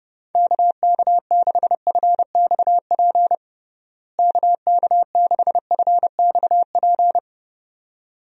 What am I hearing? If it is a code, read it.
Morse code, heard as a keyed tone, about 30 wpm, 700 Hz. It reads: KK6FXP KK6FXP
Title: USA Call Signs - 6 Characters - Repeated 2x - 30wpm